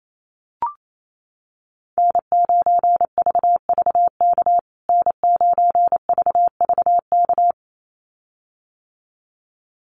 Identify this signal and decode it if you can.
Morse code, heard as a keyed tone, about 28 wpm, 700 Hz. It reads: N9VVK N9VVK
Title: USA Call Signs Repeated 2x 28wpm